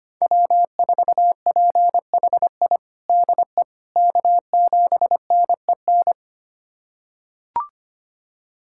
Morse code, heard as a keyed tone, about 25 wpm, 700 Hz. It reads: W4PHI DE K7NEN
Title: USA Callsign DE Callsign - 5 Characters - 25wpm